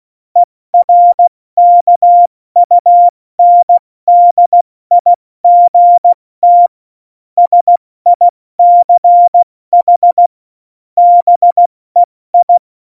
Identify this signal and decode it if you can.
Morse code, heard as a keyed tone, about 16 wpm, 700 Hz. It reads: ERKUNDIGT SICH BEI
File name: Morsekurs / Witze / Witz 02. Juli 2019, 70 F80